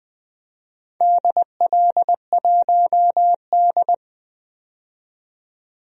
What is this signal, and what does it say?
Morse code, heard as a keyed tone, about 20 wpm, 700 Hz. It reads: DL1D
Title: DXCC-100 Call Signs - 4 Characters - 20wpm